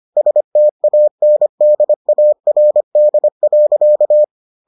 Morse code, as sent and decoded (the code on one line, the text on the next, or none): ... - .- -. -.. .- .-. -.. .-.-.-
STANDARD.